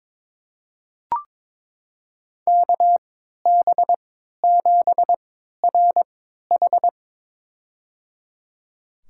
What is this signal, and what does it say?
Morse code, heard as a keyed tone, about 22 wpm, spaced as for 13 wpm, 700 Hz. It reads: KB7RH
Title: USA Call Signs - 3x Character Spacing - 22wpm